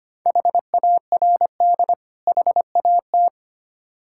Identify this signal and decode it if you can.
Morse code, heard as a keyed tone, about 25 wpm, 690 Hz. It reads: HARD HAT
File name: practice / 25wpm / Difficult Words with Hints - 25wpm